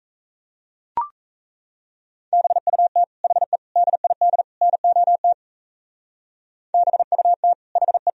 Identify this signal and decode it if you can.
Morse code, heard as a keyed tone, about 34 wpm, 700 Hz. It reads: BUT HE DID NOT BUT HE
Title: Speed-Racing - Sets of 4 Words 28wpm